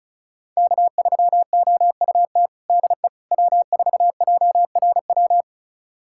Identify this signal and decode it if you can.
Morse code, heard as a keyed tone, about 36 wpm, 700 Hz. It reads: K3OUT DE W4JRW
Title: USA Callsign DE Callsign - 5 Characters - 35wpm